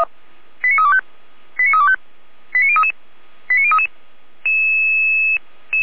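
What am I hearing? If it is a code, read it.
Five-tone selective call (ZVEI), heard as quick runs of five tones, tones of about 70 ms. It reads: 89325 89325 89033 89033
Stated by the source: ZVEI five-tone call 89001 received on SDR by feuerwehr-orlamuende (fire brigade Orlamünde).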